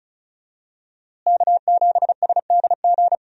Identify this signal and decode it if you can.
Morse code, heard as a keyed tone, about 36 wpm, 700 Hz. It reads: K7SDG
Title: USA Call Signs - 5 Characters - Repeated 3x - 35wpm